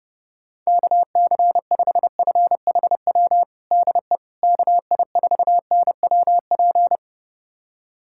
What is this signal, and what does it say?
Morse code, heard as a keyed tone, about 32 wpm, 700 Hz. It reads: KC5FHW DE KI4NWP
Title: USA Callsign DE Callsign 30wpm